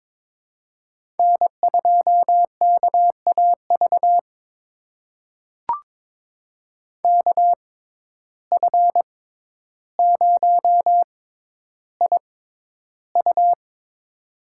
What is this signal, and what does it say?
Morse code, heard as a keyed tone, about 22 wpm, spaced as for 11 wpm, 700 Hz. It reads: N2KAV KF0IU
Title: USA Call Signs - 5 Characters - 6x Character Spacing - 22wpm